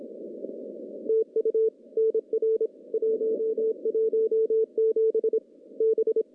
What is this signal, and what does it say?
Morse code, heard as a keyed tone, about 26 wpm, 440 Hz. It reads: TU NR 117 B
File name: contests / CW-SS-k5tr- -0200